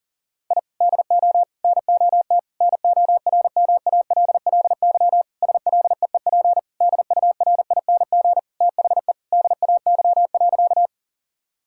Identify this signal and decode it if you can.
Morse code, heard as a keyed tone, about 40 wpm, 700 Hz. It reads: I DO NOT NORMALLY SLEEP DURING THE DAY.
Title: Sentences from Top 1000 Words - Review Part 6 - 40wpm